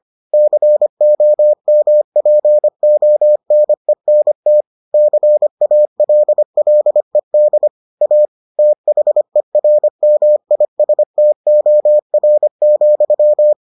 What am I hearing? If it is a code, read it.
Morse code, heard as a keyed tone, about 25 wpm, 600 Hz. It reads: COMPONENT CALLED A THERMISTOR,